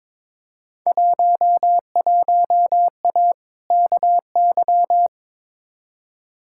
Morse code, heard as a keyed tone, about 22 wpm, 700 Hz. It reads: 11A KY
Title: Field Day Exchanges Encore - 22wpm